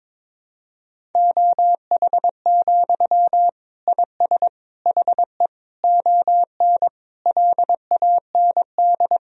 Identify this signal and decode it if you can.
Morse code, heard as a keyed tone, about 22 wpm, 700 Hz. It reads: OH, IS HE ON LAND
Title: Sentences from Top 1000 Words - Review Part 1 - 22wpm